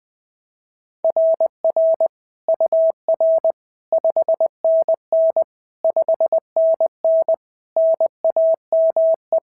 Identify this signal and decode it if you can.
Morse code, heard as a keyed tone, about 21 wpm, 690 Hz. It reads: RR UR 5NN 5NN NAME
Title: SOTA QSO Practice 20wpm